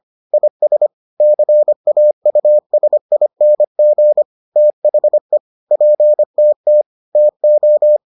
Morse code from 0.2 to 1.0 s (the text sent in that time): IS